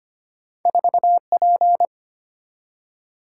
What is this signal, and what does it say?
Morse code, heard as a keyed tone, about 25 wpm, 690 Hz. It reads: 4P
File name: Two Letter-Number Combinations - 25wpm